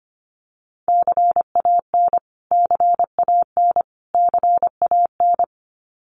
Morse code, heard as a keyed tone, about 25 wpm, 700 Hz. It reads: CAN CAN CAN
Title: Top 500 Words Repeated 25wpm